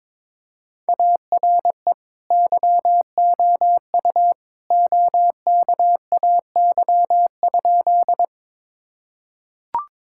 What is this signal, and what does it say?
Morse code, heard as a keyed tone, about 22 wpm, 700 Hz. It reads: ARE YOU OKAY?
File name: practice / 22wpm / Sentences with 2-Word Phrases - 22wpm